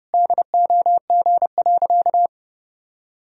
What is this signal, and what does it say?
Morse code, heard as a keyed tone, about 30 wpm, 700 Hz. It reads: DOG.
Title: Sentences from Top 1000 Words - Review Part 2 - 30wpm